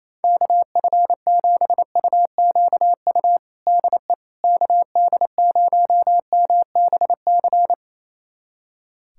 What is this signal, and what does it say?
Morse code, heard as a keyed tone, about 27 wpm, 700 Hz. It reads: KF7UQU DE KD0MBC
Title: USA Callsign DE Callsign - 6 Characters - 28wpm